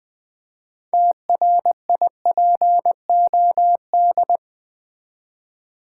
Morse code, heard as a keyed tone, about 20 wpm, 700 Hz. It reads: TRIPOD
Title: Easy Words with Hints - 20wpm